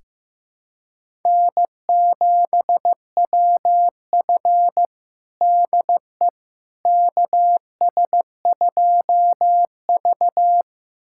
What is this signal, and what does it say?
Morse code, heard as a keyed tone, about 15 wpm, 700 Hz. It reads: N7WF DE KS2V
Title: USA Callsign DE Callsign - 4 Characters - 15wpm